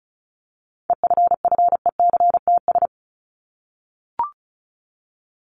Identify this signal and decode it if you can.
Morse code, heard as a keyed tone, about 34 wpm, 700 Hz. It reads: EFFECTS
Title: ES:S Suffix - Top 100 Words with Base Words 35wpm